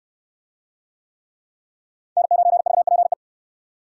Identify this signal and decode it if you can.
Morse code, heard as a keyed tone, about 65 wpm, 700 Hz. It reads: N9VPE